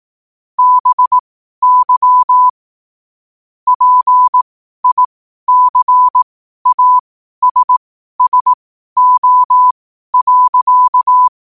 Morse code, heard as a keyed tone, about 18 wpm, spaced as for 12 wpm, 1000 Hz. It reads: BY PICASSO.
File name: qotdfiles / 12wpm